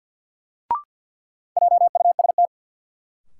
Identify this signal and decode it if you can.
Morse code, heard as a keyed tone, about 53 wpm, 700 Hz. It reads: JUST
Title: Sentences from Top 200 Words - New Word Drill - 50wpm